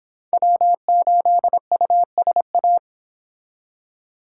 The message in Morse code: W8USA

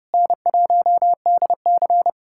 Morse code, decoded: N1DC

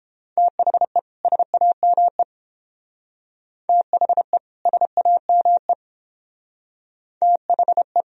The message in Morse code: THE SAME THE SAME THE

